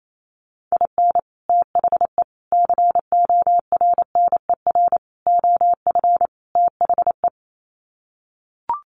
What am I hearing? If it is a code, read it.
Morse code, heard as a keyed tone, about 28 wpm, 700 Hz. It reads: IN THE CORNER OF THE
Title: Sets of 5 Words Encore 28wpm